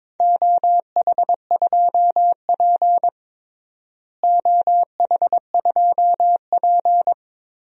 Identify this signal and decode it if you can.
Morse code, heard as a keyed tone, about 22 wpm, 700 Hz. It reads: OH2P OH2P
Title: DXCC-100 Call Signs - 4 Characters - Repeated 3x - 22wpm